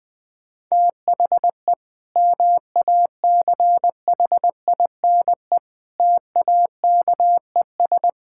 Morse code, heard as a keyed tone, about 19 wpm, 700 Hz. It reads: THE MACHINE TAKES
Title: Sentences from Top 1000 Words - Review Part 6 - 20wpm